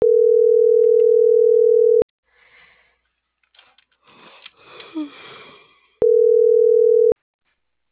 Ringback tone (North America), 440 + 480 Hz.